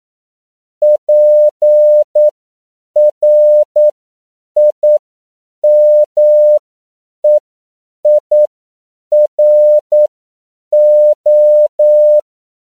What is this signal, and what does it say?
Morse code, heard as a keyed tone, about 9 wpm, spaced as for 7 wpm, 600 Hz. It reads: PRIMEIRO
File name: CW / 10PPM